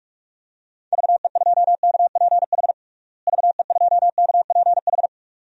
Morse code, heard as a keyed tone, about 46 wpm, 700 Hz. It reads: VE2KPH VE2KPH
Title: DXCC-100 Call Signs - 6 Characters - Repeated 2x - 45wpm